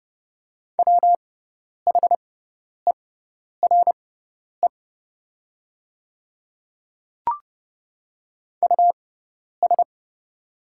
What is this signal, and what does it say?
Morse code, heard as a keyed tone, about 30 wpm, spaced as for 10 wpm, 700 Hz. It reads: WHERE US
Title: 6x Character Spacing - Top 200 Words - 30wpm